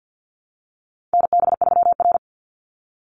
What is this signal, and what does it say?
Morse code, heard as a keyed tone, about 50 wpm, 700 Hz. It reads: NB3R
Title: Call Signs - 50wpm